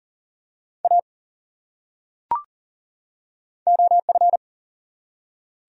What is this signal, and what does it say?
Morse code, heard as a keyed tone, about 40 wpm, 700 Hz. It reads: A OF
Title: Top 100 Words 40wpm